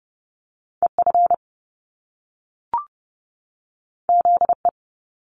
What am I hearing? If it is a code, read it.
Morse code, heard as a keyed tone, about 30 wpm, 700 Hz. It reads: EF ZE
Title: Top Two Letter Combinations in Words 30wpm